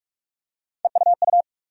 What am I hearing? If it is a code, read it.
Morse code, heard as a keyed tone, about 46 wpm, 700 Hz. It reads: EUU
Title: USA Call Sign Postfixes - 3 Letters 45wpm